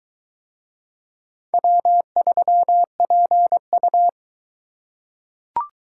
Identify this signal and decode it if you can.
Morse code, heard as a keyed tone, about 22 wpm, 700 Hz. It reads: W3PU